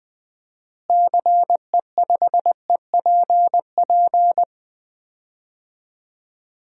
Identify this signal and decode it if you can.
Morse code, heard as a keyed tone, about 20 wpm, 700 Hz. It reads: CE5EPP
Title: DXCC-100 Call Signs - 6 Characters - Repeated 3x - 20wpm